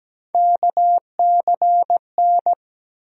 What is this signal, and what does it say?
Morse code, heard as a keyed tone, about 17 wpm, 700 Hz. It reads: KCN
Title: USA Call Sign Postfixes - 3 Letters 17wpm